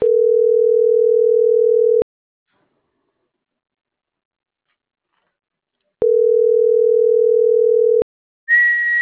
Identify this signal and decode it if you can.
Ringback tone (North America), 440 + 480 Hz.